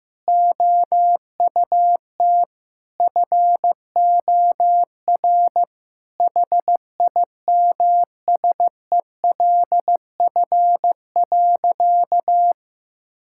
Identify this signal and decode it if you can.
Morse code, heard as a keyed tone, about 15 wpm, 700 Hz. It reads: OUT FOR HIMSELF.
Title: Sentences from Top 1600 Words 15wpm